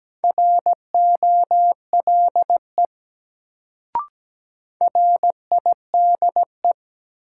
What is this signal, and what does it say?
Morse code, heard as a keyed tone, about 17 wpm, 700 Hz. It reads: ROLE RIDE
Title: Sentences from Top 1000 Words - Review Part 6 - New Word Drill - 17wpm